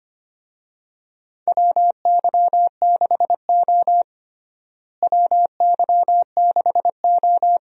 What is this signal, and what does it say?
Morse code, heard as a keyed tone, about 25 wpm, 700 Hz. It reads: WY6O WY6O
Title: USA Call Signs - 4 Characters - Repeated 2x - 25wpm